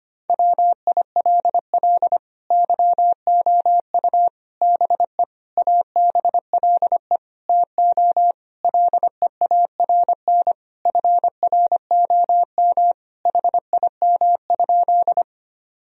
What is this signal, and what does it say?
Morse code, heard as a keyed tone, about 25 wpm, 700 Hz. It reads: WILL YOU BE ABLE TO LEARN FROM HIM?
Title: Sentences from Top 1000 Words - Review Part 2 - 25wpm